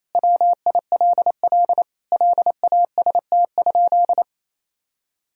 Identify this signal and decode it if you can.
Morse code, heard as a keyed tone, about 28 wpm, 700 Hz. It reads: WILL LAST?
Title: Sentences from Top 1000 Words - Review Part 2 - 28wpm